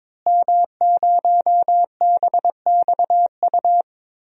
Morse code, heard as a keyed tone, about 21 wpm, 700 Hz. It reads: M0BXU
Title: DXCC-100 Call Signs - 5 Characters - Repeated 3x - 22wpm